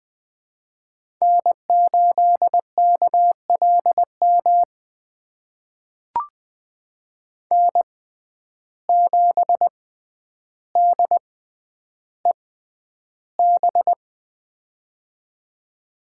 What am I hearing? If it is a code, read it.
Morse code, heard as a keyed tone, about 20 wpm, spaced as for 10 wpm, 700 Hz. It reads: N8KLM N7DEB
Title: USA Call Signs - 5 Characters - 6x Character Spacing - 20wpm